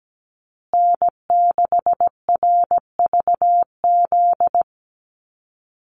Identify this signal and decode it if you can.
Morse code, heard as a keyed tone, about 17 wpm, 700 Hz. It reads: N6RVZ
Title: USA Call Signs Repeated 3x 17wpm